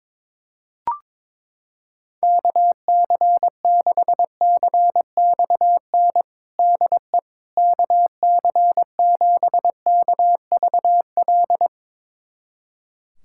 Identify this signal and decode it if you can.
Morse code, heard as a keyed tone, about 22 wpm, 700 Hz. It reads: KC6CXN DE KC7KVL
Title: USA Callsign DE Callsign - 6 Characters - 22wpm